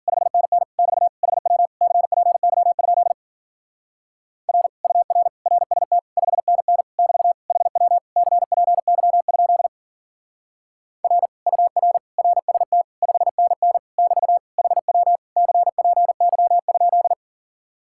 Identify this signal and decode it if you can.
Morse code, heard as a keyed tone, about 49 wpm, 700 Hz. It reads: 5NN <BT> HW CPY? R UR RST 5NN <BT> HW CPY? R UR RST 5NN <BT> HW CPY?
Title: Speed-Racing Common QSO Phrases - 40wpm